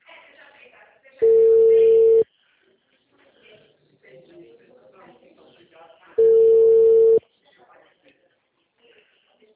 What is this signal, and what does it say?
Ringback tone (China, 450 Hz), with speech.
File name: out- -2009- - - .16978